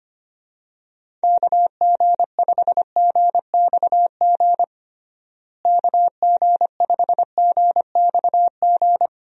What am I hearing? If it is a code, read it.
Morse code, heard as a keyed tone, about 25 wpm, 700 Hz. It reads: KG5GXG KG5GXG
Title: USA Call Signs - 6 Characters - Repeated 2x - 25wpm